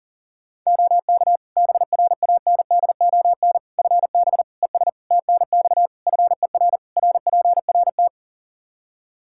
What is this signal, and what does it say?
Morse code, heard as a keyed tone, about 40 wpm, 700 Hz. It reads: OK BRANDON FB ES TNX FER RPRT
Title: Practice QSO - Exchange 2 - Info - 40wpm